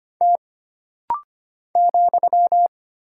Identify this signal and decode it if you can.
Morse code, heard as a keyed tone, about 25 wpm, 700 Hz. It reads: T ,
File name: Single Character (Letter:Number:Punctuation:Prosign) - Rapid-Fire - 25wpm